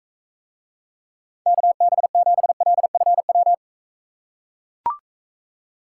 Morse code, heard as a keyed tone, about 44 wpm, 700 Hz. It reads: KB7LFW